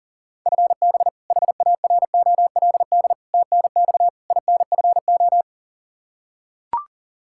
Morse code, heard as a keyed tone, about 37 wpm, 700 Hz. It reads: FB HAROLD TNX INFO
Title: Practice QSO - Exchange 3 - More Info - 40wpm